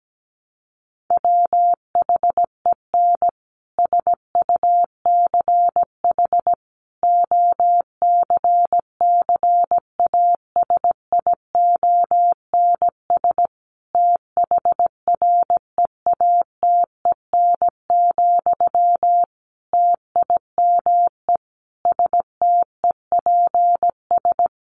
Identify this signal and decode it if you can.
Morse code, heard as a keyed tone, about 17 wpm, 700 Hz. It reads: WHEN SUCH OCCASIONS THREATEN, TIME STEPS
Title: A Sound Of Thunder 17wpm